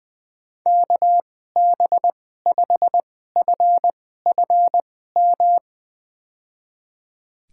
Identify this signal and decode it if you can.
Morse code, heard as a keyed tone, about 20 wpm, spaced as for 14 wpm, 700 Hz. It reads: KB5FFM
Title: USA Call Signs - 6 Characters - 2x Character Spacing - 20wpm